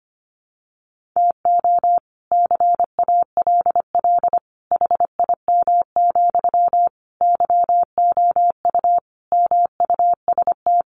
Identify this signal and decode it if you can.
Morse code, heard as a keyed tone, about 25 wpm, 700 Hz. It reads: TO CALL HIM, YOU MUST
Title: Sentences from Top 1400 Words 25wpm